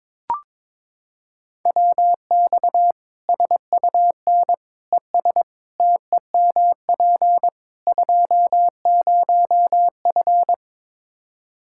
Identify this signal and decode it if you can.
Morse code, heard as a keyed tone, about 22 wpm, 700 Hz. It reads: WX SUN ES TEMP 20F
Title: Practice QSO - Exchange 2 - Info - 22wpm